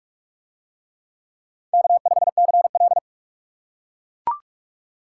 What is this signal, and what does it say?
Morse code, heard as a keyed tone, about 41 wpm, 700 Hz. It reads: K5CL